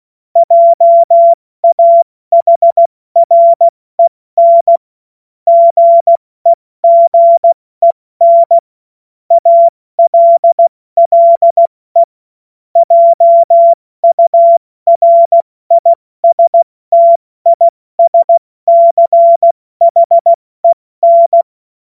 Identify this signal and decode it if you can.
Morse code, heard as a keyed tone, about 16 wpm, 680 Hz. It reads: JAHREN GEGEN ALLE JURISTISCHEN
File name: Morsekurs / Witze / Witz 25. Oktober 2022 70 F80